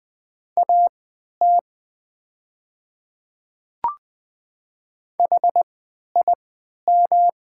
Morse code, heard as a keyed tone, about 20 wpm, spaced as for 11 wpm, 700 Hz. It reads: AT HIM
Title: 3x Character Spacing - Top 100 Words - 20wpm